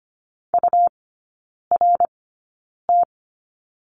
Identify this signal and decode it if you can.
Morse code, heard as a keyed tone, about 25 wpm, spaced as for 9 wpm, 700 Hz. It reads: URT
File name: Lesson 17e - Letter u - Three Characters 25wpm